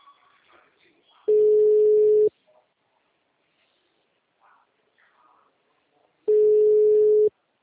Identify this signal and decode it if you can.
Ringback tone (Europe, 425 Hz), with speech.
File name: out- -2010- - - .40201